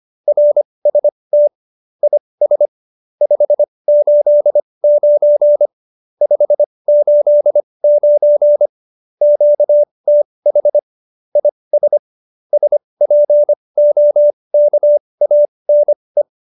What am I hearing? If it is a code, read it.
Morse code, heard as a keyed tone, about 25 wpm, spaced as for 20 wpm, 590 Hz. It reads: RST IS 589 589 QTH IS SPOKANE